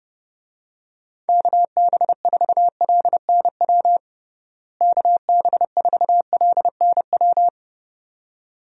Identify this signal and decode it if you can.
Morse code, heard as a keyed tone, about 32 wpm, 700 Hz. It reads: KB4LNW KB4LNW
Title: USA Call Signs - 6 Characters - Repeated 2x - 30wpm